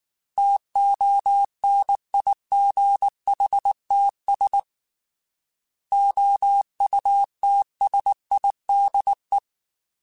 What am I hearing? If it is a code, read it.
Morse code, heard as a keyed tone, about 19 wpm, 800 Hz. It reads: TONIGHTS OUTSIDE
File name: Weather Sets / Weather Sets-mp3s / Weather 5 12 @18